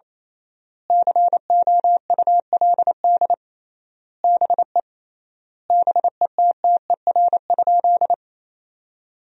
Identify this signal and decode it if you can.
Morse code, heard as a keyed tone, about 28 wpm, 700 Hz. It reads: COULD BE BETTER?
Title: 3x Word-Spacing - Sentences from Top 400 Words - 28wpm